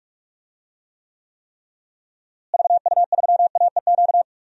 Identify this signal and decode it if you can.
Morse code, heard as a keyed tone, about 46 wpm, 700 Hz. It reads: VU3AEQ
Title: DXCC-100 Call Signs - 6 Characters - Repeated 3x - 45wpm